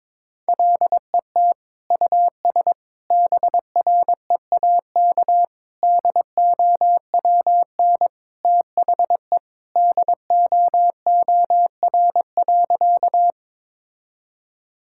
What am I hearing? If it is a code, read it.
Morse code, heard as a keyed tone, about 22 wpm, 700 Hz. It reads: LET US BREAK DOWN THE DOOR.
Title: Sentences from Top 1000 Words - Review Part 6 - 22wpm